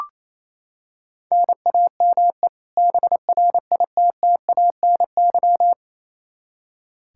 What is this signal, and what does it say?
Morse code, heard as a keyed tone, about 28 wpm, 700 Hz. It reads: NAME BRITTANY
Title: K1USN SST QSO Practice 28wpm